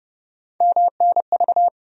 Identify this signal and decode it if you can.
Morse code, heard as a keyed tone, about 31 wpm, 700 Hz. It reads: MNV
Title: USA Call Sign Postfixes - 3 Letters 30wpm